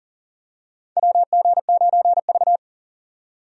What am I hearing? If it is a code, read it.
Morse code, heard as a keyed tone, about 40 wpm, 700 Hz. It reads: WG9V
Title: USA Call Signs - 4 Characters - 40wpm